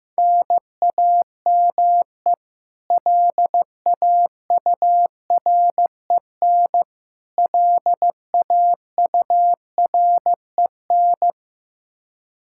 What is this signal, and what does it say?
Morse code, heard as a keyed tone, about 15 wpm, 700 Hz. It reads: NAME LAUREN LAUREN
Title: Practice QSO - Exchange 1 - Beginning - 15wpm